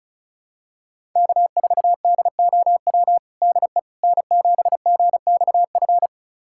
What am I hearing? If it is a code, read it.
Morse code, heard as a keyed tone, about 34 wpm, 700 Hz. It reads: K4DOW DE N7GXF
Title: USA Callsign DE Callsign - 5 Characters - 35wpm